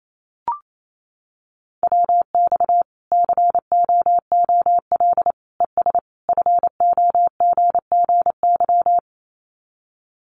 Morse code, heard as a keyed tone, about 28 wpm, 700 Hz. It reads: WX COOL ES FOGGY
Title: Common QSO Phrases 28wpm